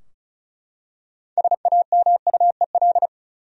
Morse code, heard as a keyed tone, about 34 wpm, 700 Hz. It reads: SAMUEL
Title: USA Common Names - 35wpm